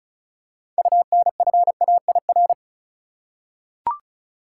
Morse code, heard as a keyed tone, about 34 wpm, 700 Hz. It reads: UNFAIR
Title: Six Letter Words - 35wpm